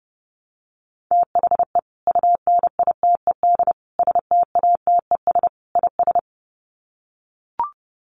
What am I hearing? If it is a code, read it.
Morse code, heard as a keyed tone, about 28 wpm, 700 Hz. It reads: THE UNITED STATES IS
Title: Sets of 4 Words 30wpm